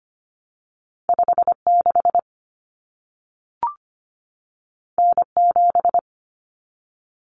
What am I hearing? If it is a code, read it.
Morse code, heard as a keyed tone, about 25 wpm, 700 Hz. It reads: 56 N7
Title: Common Cut Numbers - 2-Digit 25wpm